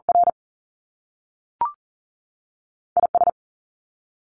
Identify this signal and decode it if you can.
Morse code, heard as a keyed tone, about 40 wpm, 700 Hz. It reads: R IS
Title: Top Two Letter Combinations in Words 40wpm